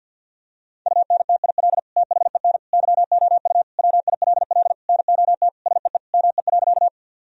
Morse code, heard as a keyed tone, about 52 wpm, 700 Hz. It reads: UNTIL THEN YOU WILL NOT SEE ME.
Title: Sentences from Top 1000 Words - Review Part 1 - 50wpm